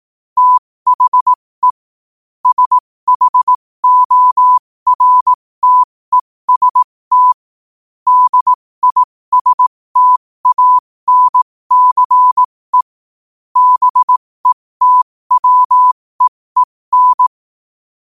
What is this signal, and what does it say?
Morse code, heard as a keyed tone, about 18 wpm, spaced as for 15 wpm, 1000 Hz. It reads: THE SHORTEST DISTANCE BETWEEN
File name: qotdfiles / 15wpm